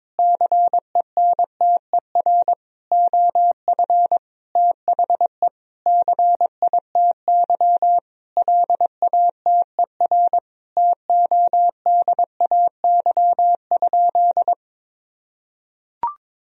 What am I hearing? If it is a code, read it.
Morse code, heard as a keyed tone, about 22 wpm, 700 Hz. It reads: CENTER OF THE CITY LATER TODAY?
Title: Sentences from Top 1000 Words - Review Part 2 - 22wpm